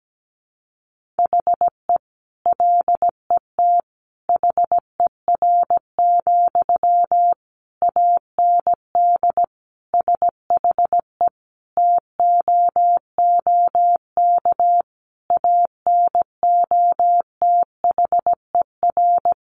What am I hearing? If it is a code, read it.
Morse code, heard as a keyed tone, about 17 wpm, 700 Hz. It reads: HE LET HER, AND SHE TOOK ANOTHER
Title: Sentences from Top 1500 Words 17wpm